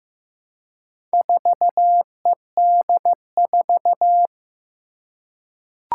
Morse code, heard as a keyed tone, about 15 wpm, 700 Hz. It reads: 4ED4
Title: Four Challenging Letter-Number Combinations 15wpm